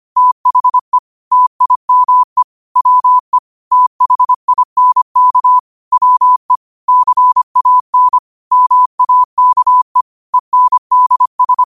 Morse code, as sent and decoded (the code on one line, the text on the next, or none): - .... / - .. -- . / .-- . / - .... .. -. -.- / .-- . / -.-. .- -. / -- .- -.- . / . -. -.. ...
THE TIME WE THINK WE CAN MAKE ENDS